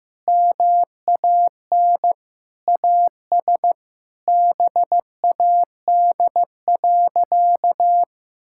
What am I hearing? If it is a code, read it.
Morse code, heard as a keyed tone, about 15 wpm, 700 Hz. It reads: MAN AS BAD.
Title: Sentences from Top 1000 Words - Review Part 6 - 15wpm